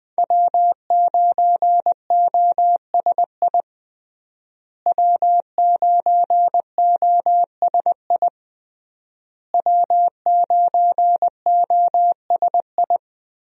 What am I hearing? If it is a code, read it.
Morse code, heard as a keyed tone, about 21 wpm, 700 Hz. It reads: W9OSI W9OSI W9OSI
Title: USA Call Signs - 5 Characters - Repeated 3x - 20wpm